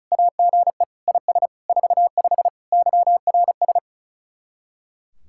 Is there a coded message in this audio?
Yes: Morse code from 0.1 to 3.8 s, heard as a keyed tone, about 46 wpm, 700 Hz. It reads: AGE IS 45 YRS